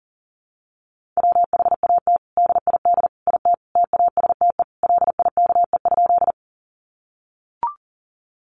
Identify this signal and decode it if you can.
Morse code, heard as a keyed tone, about 40 wpm, 700 Hz. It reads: WHAT DID IT TASTE LIKE?
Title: Sentences from Top 1100 Words 40wpm